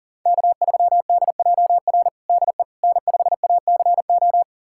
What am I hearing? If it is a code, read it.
Morse code, heard as a keyed tone, about 37 wpm, 700 Hz. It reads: K3DJR DE N5ACO
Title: USA Callsign DE Callsign - 5 Characters - 40wpm